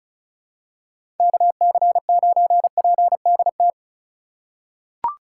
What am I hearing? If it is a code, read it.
Morse code, heard as a keyed tone, about 36 wpm, 700 Hz. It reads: KC9PDT
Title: USA Call Signs - 8x Character Spacing - 35wpm